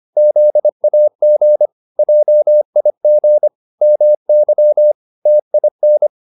Morse code, heard as a keyed tone, about 25 wpm, 600 Hz. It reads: ZAG JIG MY TIN